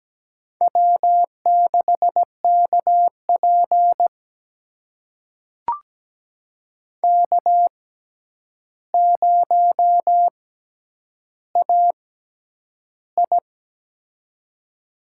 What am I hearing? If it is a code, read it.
Morse code, heard as a keyed tone, about 17 wpm, spaced as for 9 wpm, 700 Hz. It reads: W6KP K0AI
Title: USA Call Signs - 4 Characters - 6x Character Spacing - 17wpm